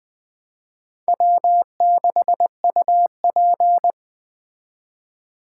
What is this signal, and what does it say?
Morse code, heard as a keyed tone, about 20 wpm, 700 Hz. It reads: W6UP
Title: Speed Racing - USA 1x2 Call Signs 17wpm